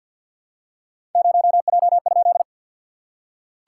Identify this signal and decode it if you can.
Morse code, heard as a keyed tone, about 47 wpm, 660 Hz. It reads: 0J?
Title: SOTA QSO Practice 50wpm